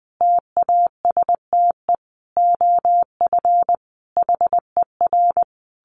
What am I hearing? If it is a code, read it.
Morse code, heard as a keyed tone, about 20 wpm, 700 Hz. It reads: TASTE OF HER
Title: Sentences from Top 1100 Words 20wpm